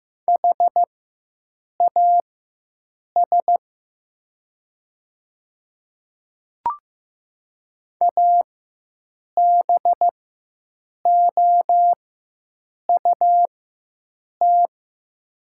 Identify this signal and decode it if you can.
Morse code, heard as a keyed tone, about 15 wpm, spaced as for 7 wpm, 700 Hz. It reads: HAS ABOUT
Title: 4x Character Spacing - Top 200 Words - 15wpm